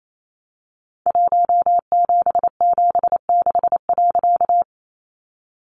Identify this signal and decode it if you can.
Morse code, heard as a keyed tone, about 28 wpm, 700 Hz. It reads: 1776.
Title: A Sound Of Thunder 28wpm